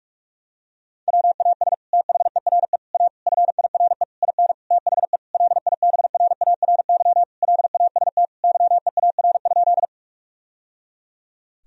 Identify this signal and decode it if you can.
Morse code, heard as a keyed tone, about 46 wpm, 700 Hz. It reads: WAS THERE A FIRE IN THE LIBRARY LAST YEAR?
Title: Sentences from Top 1000 Words - Review Part 4 - 45wpm